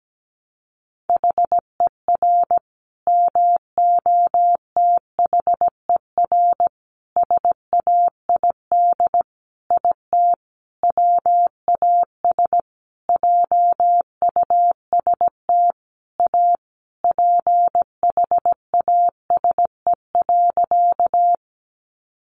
Morse code, heard as a keyed tone, about 17 wpm, 700 Hz. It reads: HER MOTHER SAID IT WAS JUST A PHASE.
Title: Sentences from Top 1200 Words 17wpm